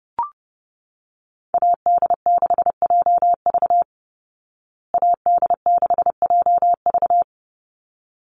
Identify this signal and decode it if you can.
Morse code, heard as a keyed tone, about 28 wpm, 700 Hz. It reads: AD6JV AD6JV
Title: Call Signs Repeated 3x - 30wpm